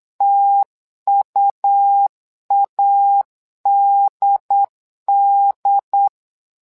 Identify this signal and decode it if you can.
Morse code, heard as a keyed tone, about 8 wpm, 800 Hz. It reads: TUADD
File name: LEZIONI / 2 UHDN / 2 UHDN 8wpm2232